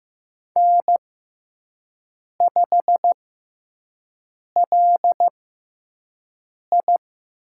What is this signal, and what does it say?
Morse code, heard as a keyed tone, about 15 wpm, spaced as for 5 wpm, 700 Hz. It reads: N5LI
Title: USA Call Signs - 6x Character Spacing - 15wpm